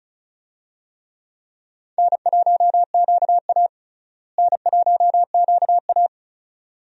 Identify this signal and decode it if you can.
Morse code, heard as a keyed tone, about 36 wpm, 700 Hz. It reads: N1QA N1QA
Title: USA Call Signs - 4 Characters - Repeated 2x - 35wpm